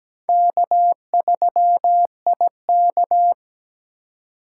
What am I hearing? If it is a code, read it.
Morse code, heard as a keyed tone, about 17 wpm, 700 Hz. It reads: K3IK
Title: USA Call Signs - 4 Characters - 2x Character Spacing - 17wpm